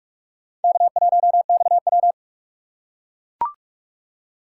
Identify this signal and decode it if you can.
Morse code, heard as a keyed tone, about 44 wpm, 700 Hz. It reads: K1XW